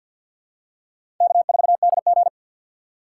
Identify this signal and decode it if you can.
Morse code, heard as a keyed tone, about 49 wpm, 700 Hz. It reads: K4DG
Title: USA Call Signs - 4 Characters - 2x Character Spacing - 50wpm